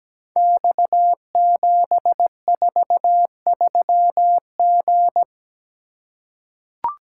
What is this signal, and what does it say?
Morse code, heard as a keyed tone, about 17 wpm, 700 Hz. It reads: X743G
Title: Code Groups - 5 Letter:Numbers - 17wpm